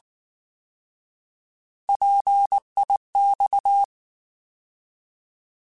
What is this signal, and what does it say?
Morse code, heard as a keyed tone, about 19 wpm, 800 Hz. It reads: PIX
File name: Alphabetical / Alphabetical mp3s / Ps @18